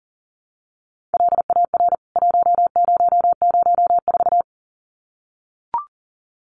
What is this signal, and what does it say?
Morse code, heard as a keyed tone, about 40 wpm, 700 Hz. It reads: LAR 1004
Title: CWT Exchange 40wpm